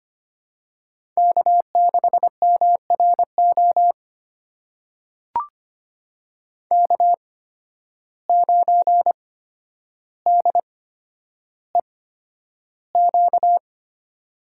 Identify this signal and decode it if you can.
Morse code, heard as a keyed tone, about 25 wpm, spaced as for 11 wpm, 700 Hz. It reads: K6MRO K9DEQ
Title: USA Call Signs - 8x Character Spacing - 25wpm